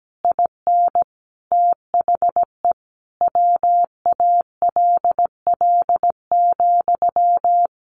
Morse code, heard as a keyed tone, about 17 wpm, 700 Hz. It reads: IN THE WALL,
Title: A Sound Of Thunder 17wpm